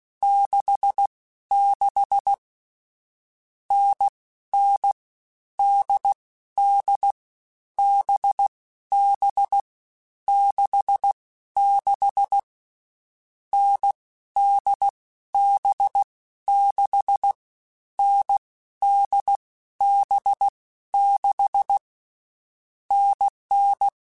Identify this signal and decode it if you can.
Morse code, heard as a keyed tone, about 16 wpm, 800 Hz. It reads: 6 6 N N D D B B 6 6 N D B 6 N D B 6 NN
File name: Short Runs / Short Runs-mp3s / NDB NDB6 @15